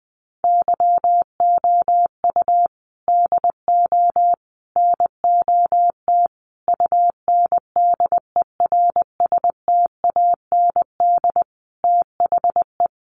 Morse code, heard as a keyed tone, about 19 wpm, 700 Hz. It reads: YOU DO NOT UNDERSTAND THE
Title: Sentences from Top 600 Words 20wpm